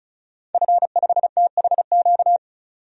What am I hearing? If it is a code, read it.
Morse code, heard as a keyed tone, about 34 wpm, 700 Hz. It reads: F5THQ
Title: DXCC-100 Call Signs - 5 Characters - 35wpm